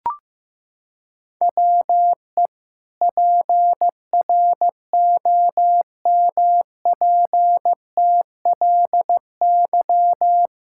Morse code, heard as a keyed tone, about 16 wpm, 700 Hz. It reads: WE PROMPTLY
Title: Pangram Challenge v2 - 15wpm